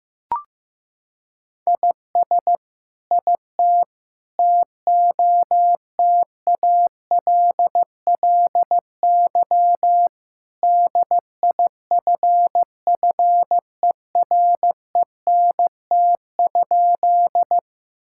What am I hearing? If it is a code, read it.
Morse code, heard as a keyed tone, about 15 wpm, 700 Hz. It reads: IS IT TOTALLY DIFFERENT?
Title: Sentences from Top 1600 Words 15wpm